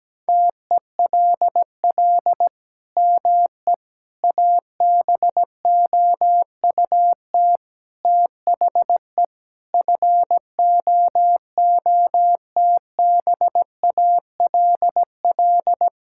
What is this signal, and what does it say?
Morse code, heard as a keyed tone, about 17 wpm, 700 Hz. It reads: TELL ME ABOUT THE FOOTBALL
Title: Sentences from Top 1600 Words 17wpm